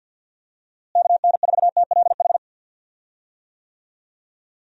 Morse code, heard as a keyed tone, about 51 wpm, 700 Hz. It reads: KN4TLH
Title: USA Call Signs - 6 Characters - Repeated 3x - 50wpm